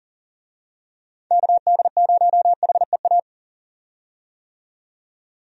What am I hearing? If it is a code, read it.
Morse code, heard as a keyed tone, about 40 wpm, 690 Hz. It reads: KD0HEA
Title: USA Call Signs - 6 Characters - 40wpm